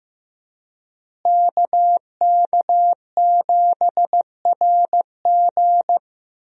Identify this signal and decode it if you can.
Morse code, heard as a keyed tone, about 16 wpm, 700 Hz. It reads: KK7RG
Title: USA Call Signs - 5 Characters - Repeated 3x - 15wpm